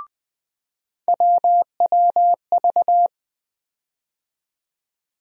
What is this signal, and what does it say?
Morse code, heard as a keyed tone, about 20 wpm, 700 Hz. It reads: WWV
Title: USA Call Sign Postfixes - 3 Letters 20wpm